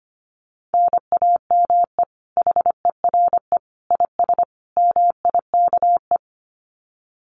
Morse code, heard as a keyed tone, about 25 wpm, 700 Hz. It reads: NAME HERE IS MIKE
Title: Common QSO Phrases 25wpm